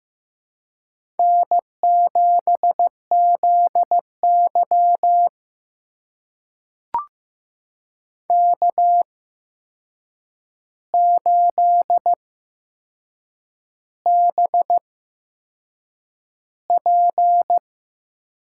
Morse code, heard as a keyed tone, about 15 wpm, spaced as for 6 wpm, 700 Hz. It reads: N7ZY K8BP
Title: USA Call Signs - 8x Character Spacing - 15wpm